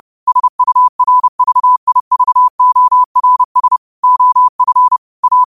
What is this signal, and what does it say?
Morse code, heard as a keyed tone, about 30 wpm, 1000 Hz. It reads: SURVIVORS OF A
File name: qotdfiles / 30wpm